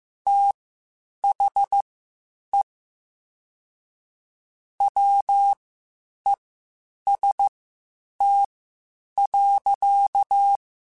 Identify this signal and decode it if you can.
Morse code, heard as a keyed tone, about 15 wpm, spaced as for 8 wpm, 800 Hz. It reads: THE WEST.
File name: Spaced Characters / Spaced Characters-mp3s / Weather5 1 sp @13